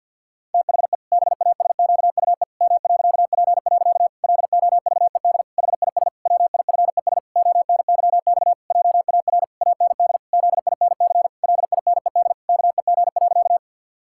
Morse code, heard as a keyed tone, about 49 wpm, 700 Hz. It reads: THE BASQUE M.P. LOVED HIS WIFES ONYX JAR AND ZINC LINED KEG.
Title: Pangram Challenge v2 - 50wpm